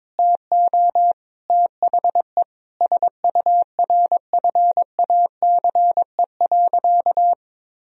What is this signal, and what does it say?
Morse code, heard as a keyed tone, about 21 wpm, 700 Hz. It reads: TO THE SURFACE.
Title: Sentences from Top 1000 Words - Review Part 3 - 22wpm